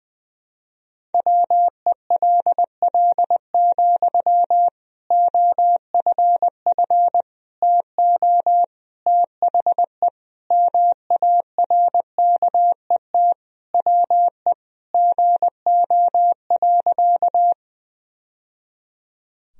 Morse code, heard as a keyed tone, about 20 wpm, 700 Hz. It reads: WELL, OFF TO THE MARKET WE GO.
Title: Sentences from Top 1000 Words - Review Part 3 - 20wpm